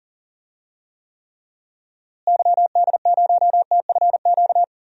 Morse code, heard as a keyed tone, about 39 wpm, 700 Hz. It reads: YD0TFQ
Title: DXCC-100 Call Signs - 6 Characters - Repeated 3x - 40wpm